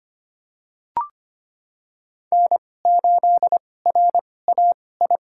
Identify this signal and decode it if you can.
Morse code, heard as a keyed tone, about 25 wpm, spaced as for 18 wpm, 700 Hz. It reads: N8RAI
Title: USA Call Signs - 5 Characters - 2x Character Spacing - 25wpm